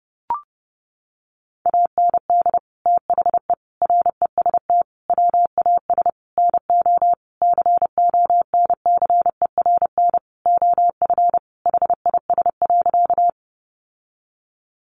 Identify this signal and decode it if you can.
Morse code, heard as a keyed tone, about 30 wpm, 700 Hz. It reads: AND THE REST WAS NO CONCERN OF HIS.
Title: Sentences from Top 1000 Words 30wpm